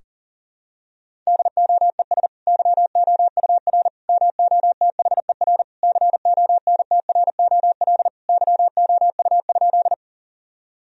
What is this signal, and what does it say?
Morse code, heard as a keyed tone, about 40 wpm, 700 Hz. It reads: DOES YOUR MOTHER CONTROL YOU?
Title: Sentences from Top 1000 Words - Review Part 3 - 40wpm